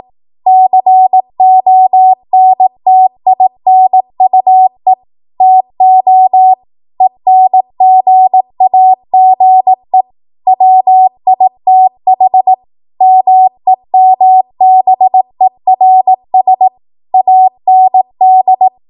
Morse code, heard as a keyed tone, about 18 wpm, 750 Hz. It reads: CONTINUE TO ENGAGE WITH MEMBERS AND
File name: Morse / 18WPM